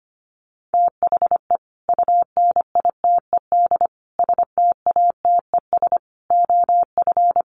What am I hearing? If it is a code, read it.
Morse code, heard as a keyed tone, about 25 wpm, 700 Hz. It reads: THE UNITED STATES OF
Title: Sets of 5 Words 25wpm